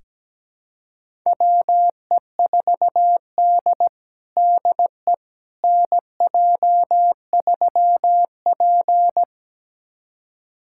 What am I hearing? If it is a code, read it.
Morse code, heard as a keyed tone, about 17 wpm, 700 Hz. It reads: WE4D DE NJ3P
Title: USA Callsign DE Callsign - 4 Characters - 17wpm